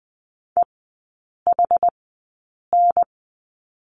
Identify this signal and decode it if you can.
Morse code, heard as a keyed tone, about 20 wpm, spaced as for 8 wpm, 700 Hz. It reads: EHN
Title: 198 Lesson 32e - Letter k - Three Characters 20wpm